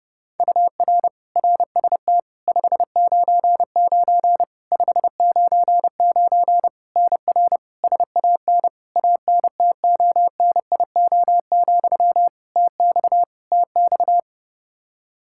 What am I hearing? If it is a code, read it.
Morse code, heard as a keyed tone, about 32 wpm, 700 Hz. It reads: UR RST 599 599 NR SAN ANTONIO, TX TX
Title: K1USN SST QSO Practice 30wpm